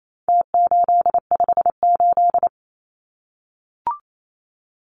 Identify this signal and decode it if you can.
Morse code, heard as a keyed tone, about 28 wpm, 700 Hz. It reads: T858
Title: Numbers - Contest Style - 4-Digit 28wpm